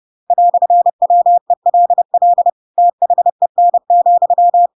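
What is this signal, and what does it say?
Morse code, heard as a keyed tone, about 30 wpm, 700 Hz. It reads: "WELL THEN,
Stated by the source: Chapter 2 of The Adventures of Pinocchio read as a book in Morse code (CW) at 30 wpm, by ik1hge.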